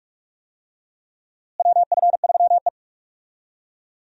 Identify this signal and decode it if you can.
Morse code, heard as a keyed tone, about 44 wpm, 700 Hz. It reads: WF3E
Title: USA Call Signs - 4 Characters - Repeated 2x - 45wpm